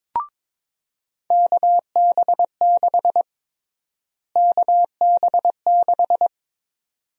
Morse code, heard as a keyed tone, about 22 wpm, 700 Hz. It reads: KB6 KB6
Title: USA Call Sign Prefixes - Repeated 2x - 22wpm